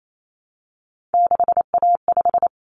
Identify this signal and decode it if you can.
Morse code, heard as a keyed tone, about 28 wpm, 700 Hz. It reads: 6A5
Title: Common Cut Numbers - 3-Digit 28wpm